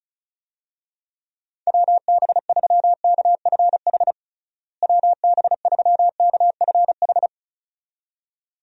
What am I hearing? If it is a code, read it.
Morse code, heard as a keyed tone, about 36 wpm, 700 Hz. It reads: WB3KFH WB3KFH
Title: USA Call Signs - 6 Characters - Repeated 2x - 35wpm